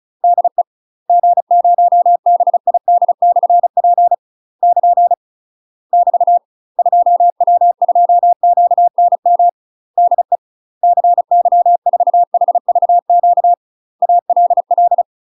Morse code, heard as a keyed tone, about 34 wpm, 700 Hz. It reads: DE G0BID/P <KN> <BT> 2W2QNM DE CY4HVQ ALL
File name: UK MASTER QSOs TRAINING / 35WPM / UK MASTER QSOs 0008